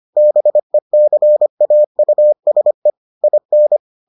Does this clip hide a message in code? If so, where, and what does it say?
Yes: Morse code from 0.2 to 3.8 s, heard as a keyed tone, about 25 wpm, 600 Hz. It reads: BECAUSE IN